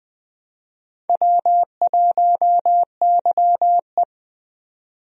Morse code, heard as a keyed tone, about 20 wpm, 700 Hz. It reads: W1YE
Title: USA Call Signs - 4 Characters - 20wpm